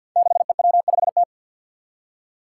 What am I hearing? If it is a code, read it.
Morse code, heard as a keyed tone, about 52 wpm, 700 Hz. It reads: 6EW5T
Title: Five Challenging Letter-Number Combinations 50wpm